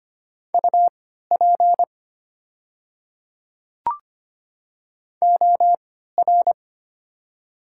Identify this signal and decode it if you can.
Morse code, heard as a keyed tone, about 25 wpm, spaced as for 14 wpm, 700 Hz. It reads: UP OR
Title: 3x Character Spacing - Top 200 Words - 25wpm